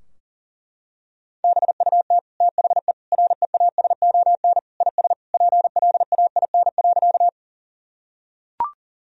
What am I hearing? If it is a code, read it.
Morse code, heard as a keyed tone, about 37 wpm, 700 Hz. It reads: BUT THE REASON IS PLAIN.
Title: Sentences from Top 1000 Words - Review Part 2 - 40wpm